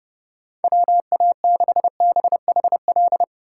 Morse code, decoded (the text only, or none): WA6BHL